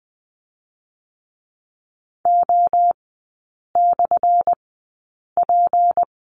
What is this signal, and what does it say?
Morse code, heard as a keyed tone, about 20 wpm, spaced as for 8 wpm, 690 Hz. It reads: O/P
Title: 205 Lesson 33e - Letter j - Three Characters 20wpm